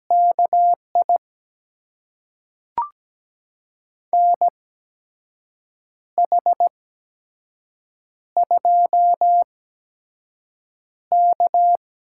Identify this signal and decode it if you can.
Morse code, heard as a keyed tone, about 17 wpm, spaced as for 6 wpm, 700 Hz. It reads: KI NH2K
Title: USA Call Signs - 4 Characters - 8x Character Spacing - 17wpm